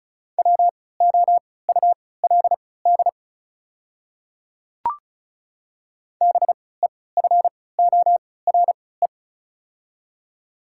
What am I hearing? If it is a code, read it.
Morse code, heard as a keyed tone, about 36 wpm, spaced as for 20 wpm, 700 Hz. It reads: WOULD BEFORE
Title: 3x Character Spacing - Top 200 Words - 35wpm